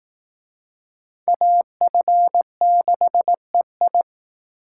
Morse code, heard as a keyed tone, about 19 wpm, 700 Hz. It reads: AF6EI
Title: Speed Racing - USA 2x2 Call Signs 15wpm